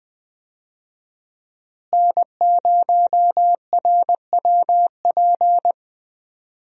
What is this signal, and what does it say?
Morse code, heard as a keyed tone, about 20 wpm, 700 Hz. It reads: N0RWP
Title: USA Call Signs - 5 Characters - Repeated 2x - 20wpm